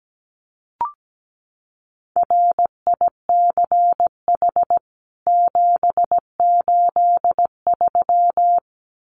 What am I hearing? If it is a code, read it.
Morse code, heard as a keyed tone, about 17 wpm, 700 Hz. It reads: RICH 783
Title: CWT Exchange 17wpm